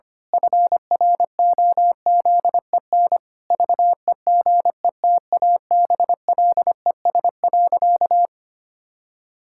Morse code, heard as a keyed tone, about 25 wpm, 700 Hz. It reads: FROZEN VEGETABLES.
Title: Pangram Challenge v2 - 25wpm